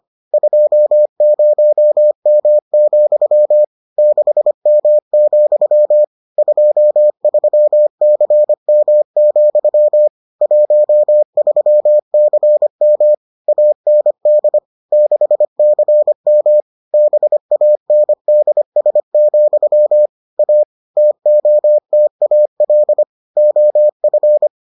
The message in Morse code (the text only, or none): 20M, 6M, 23CM, 13CM AND 6CM BANDS, A TOTAL OF